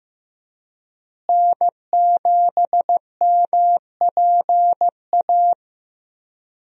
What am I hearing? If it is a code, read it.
Morse code, heard as a keyed tone, about 15 wpm, 700 Hz. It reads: N7MPA
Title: USA Call Signs - 5 Characters - Repeated 3x - 15wpm